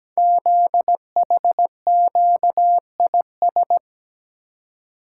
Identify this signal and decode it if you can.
Morse code, heard as a keyed tone, about 17 wpm, 700 Hz. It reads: ZHQIS
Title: Code Groups - 5 Letters - 17wpm